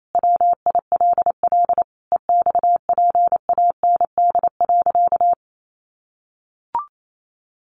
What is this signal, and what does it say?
Morse code, heard as a keyed tone, about 28 wpm, 700 Hz. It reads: WILL EXPAND.
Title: Sentences from Top 1400 Words 28wpm